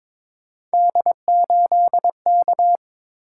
Morse code, heard as a keyed tone, about 22 wpm, 700 Hz. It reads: D8K
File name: practice / 22wpm / Three Letter-Number Combinations - 22wpm